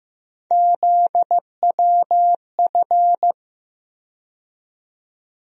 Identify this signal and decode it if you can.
Morse code, heard as a keyed tone, about 15 wpm, 700 Hz. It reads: ZWF
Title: USA Call Sign Postfixes - 3 Letters 15wpm